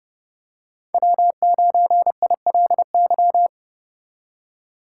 Morse code, heard as a keyed tone, about 31 wpm, 700 Hz. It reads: W9ILY
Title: K1USN SST QSO Practice - 30wpm